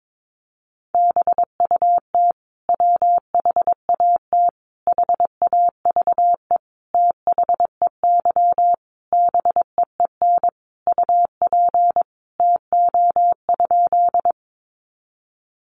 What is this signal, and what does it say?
Morse code, heard as a keyed tone, about 22 wpm, 700 Hz. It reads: BUT WHAT HAVE THEY BEEN UP TO?
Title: Sentences from Top 100 Words 22wpm